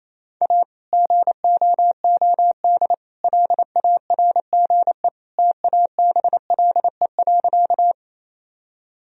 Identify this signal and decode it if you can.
Morse code, heard as a keyed tone, about 28 wpm, 700 Hz. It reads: A GOOD LARGE TABLE.
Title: Sentences from Top 1000 Words - Review Part 2 - 28wpm